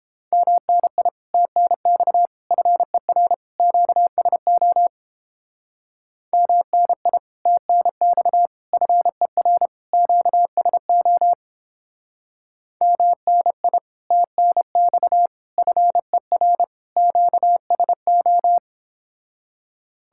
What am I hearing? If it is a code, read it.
Morse code, heard as a keyed tone, about 31 wpm, 700 Hz. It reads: MNI TNX FER QSO MNI TNX FER QSO MNI TNX FER QSO